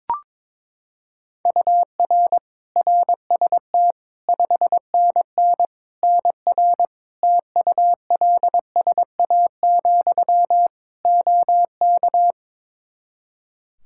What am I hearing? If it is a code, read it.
Morse code, heard as a keyed tone, about 22 wpm, 700 Hz. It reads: UR RST 5NN NR TULSA, OK
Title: Practice QSO - Exchange 1 - Beginning - 22wpm